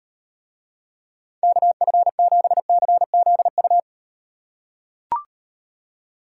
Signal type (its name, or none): Morse code